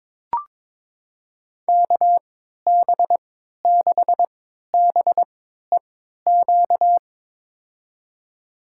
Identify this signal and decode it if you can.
Morse code, heard as a keyed tone, about 21 wpm, spaced as for 12 wpm, 700 Hz. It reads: KB6BEQ
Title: USA Call Signs - 6 Characters - 3x Character Spacing - 22wpm